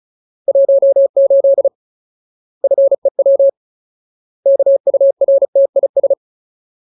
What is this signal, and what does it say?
Morse code, heard as a keyed tone, about 34 wpm, 550 Hz. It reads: 18 FEW KURTIS
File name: Ultimate Word List / 35WPM Ultimate